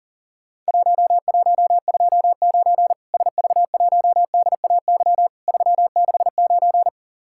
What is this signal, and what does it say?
Morse code, heard as a keyed tone, about 40 wpm, 700 Hz. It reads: 1129 SV1DAY 369
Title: Worked All Europe DX Contest - QTC Exchange - 40wpm